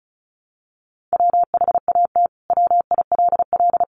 Morse code, heard as a keyed tone, about 34 wpm, 700 Hz. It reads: WHAT WILL
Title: Sentences from Top 1300 Words 35wpm